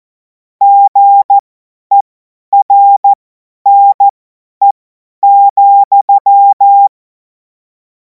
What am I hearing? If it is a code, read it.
Morse code, heard as a keyed tone, about 14 wpm, spaced as for 10 wpm, 800 Hz. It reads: GERNE,
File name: Morsekurs / Witze / Witz 17. März 2026 50 F70